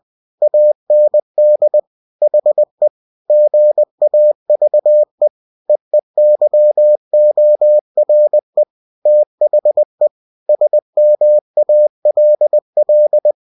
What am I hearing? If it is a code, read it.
Morse code, heard as a keyed tone, about 20 wpm, 600 Hz. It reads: AND HE GAVE EEYORE THE SMALL
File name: cw-books / winnie-the-pooh / winnie-20wpm-page-0056